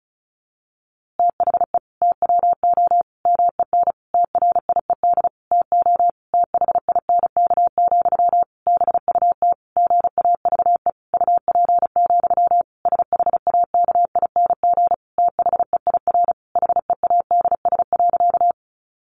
Morse code, heard as a keyed tone, about 34 wpm, 700 Hz. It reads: THE TWO MEN TRIED TO THINK, BUT GAVE UP, SHAKING THEIR HEADS.
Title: A Sound Of Thunder 35wpm